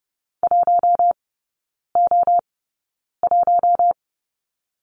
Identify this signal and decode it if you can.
Morse code, heard as a keyed tone, about 30 wpm, spaced as for 9 wpm, 700 Hz. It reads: 1O1
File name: Lesson 18e - Letter m - Three Characters 30wpm